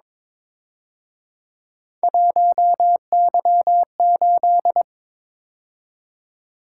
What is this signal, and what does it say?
Morse code, heard as a keyed tone, about 26 wpm, 700 Hz. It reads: 1Y8
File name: Speed-Racing - Three Letter-Number Combinations - 22wpm